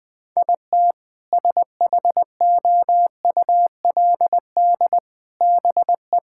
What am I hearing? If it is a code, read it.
Morse code, heard as a keyed tone, about 20 wpm, 700 Hz. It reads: IT SHOULD BE
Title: Sentences with 2-Word Phrases - 20wpm